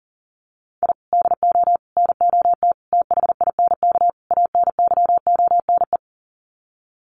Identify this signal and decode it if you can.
Morse code, heard as a keyed tone, about 39 wpm, 700 Hz. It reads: I DO NOT THINK ANYONE
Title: Sets of 5 Words Encore 40wpm